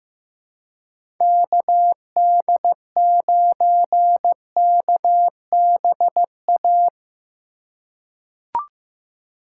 Morse code, heard as a keyed tone, about 16 wpm, 700 Hz. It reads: KD9KBA
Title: USA Call Signs - 3x Character Spacing - 15wpm